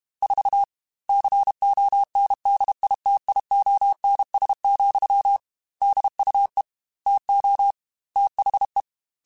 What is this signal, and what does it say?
Morse code, heard as a keyed tone, about 32 wpm, 800 Hz. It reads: 4 CONDITIONS, DUE TO THE
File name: Technical Subjects / Technical Subjects-mp3s / AND Gate @30